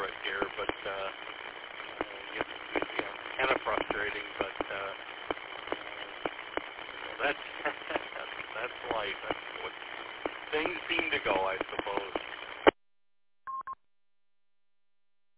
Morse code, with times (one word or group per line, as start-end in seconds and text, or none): none